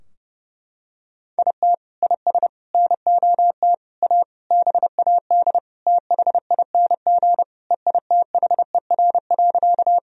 Morse code, heard as a keyed tone, about 30 wpm, 700 Hz. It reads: IT IS NOT A BAD THING EITHER.
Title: Sentences from Top 1000 Words - Review Part 4 - 30wpm